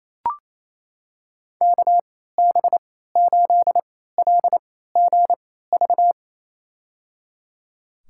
Morse code, heard as a keyed tone, about 28 wpm, spaced as for 16 wpm, 700 Hz. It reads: KB8LGV
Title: USA Call Signs - 6 Characters - 3x Character Spacing - 28wpm